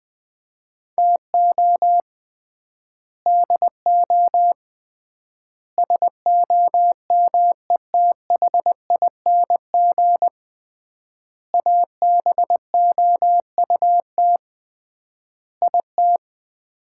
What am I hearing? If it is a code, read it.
Morse code, heard as a keyed tone, about 21 wpm, 700 Hz. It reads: TO DO SOMETHING ABOUT IT
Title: 3x Word-Spacing - Sets of 5 Words - 20wpm